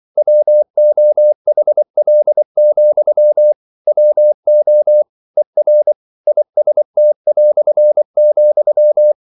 Morse code, heard as a keyed tone, about 24 wpm, 600 Hz. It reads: WOHL, WO ER IST",